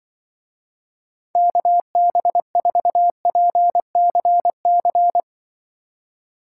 Morse code, heard as a keyed tone, about 25 wpm, 700 Hz. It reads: KB4PCC
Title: Speed Racing - USA Call Signs 20wpm